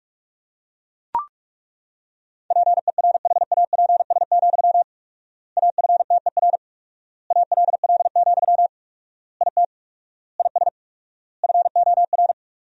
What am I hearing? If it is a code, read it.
Morse code, heard as a keyed tone, about 43 wpm, 690 Hz. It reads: PERHAPS, AFTER ALL, IT IS FOR
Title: 4x Word-Spacing - Sentences from Top 500 Words - 45wpm